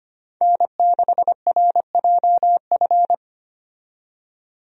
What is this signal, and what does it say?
Morse code, heard as a keyed tone, about 25 wpm, 700 Hz. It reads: N6RJF
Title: USA Call Signs - 5 Characters - 25wpm